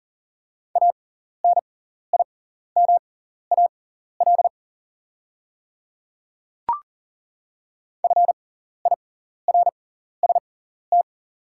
Morse code, heard as a keyed tone, about 40 wpm, spaced as for 14 wpm, 690 Hz. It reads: ANIMAL FIRST
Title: 6x Character Spacing - Top 200 Words - 40wpm